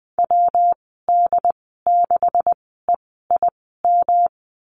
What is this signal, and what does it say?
Morse code, heard as a keyed tone, about 20 wpm, spaced as for 14 wpm, 700 Hz. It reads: WD6EIM
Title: 128 Lesson 21g - Number 6 - Call Signs 20wpm